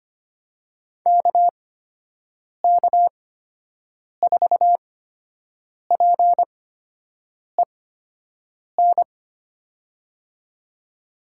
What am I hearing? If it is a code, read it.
Morse code, heard as a keyed tone, about 25 wpm, spaced as for 7 wpm, 700 Hz. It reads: KK4PEN
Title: USA Call Signs - 8x Character Spacing - 25wpm